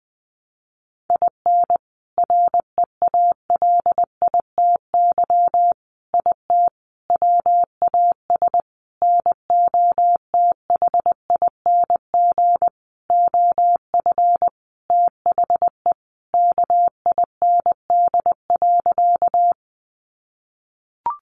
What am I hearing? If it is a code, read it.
Morse code, heard as a keyed tone, about 20 wpm, 700 Hz. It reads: IN REALITY IT WAS NOTHING OF THE KIND.
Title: Sentences from Top 1200 Words 20wpm